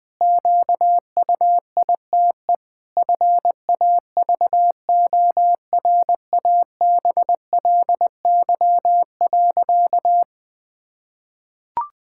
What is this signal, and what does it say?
Morse code, heard as a keyed tone, about 21 wpm, 700 Hz. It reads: QUITE FAVORABLY.
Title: Pangram Challenge v2 - 20wpm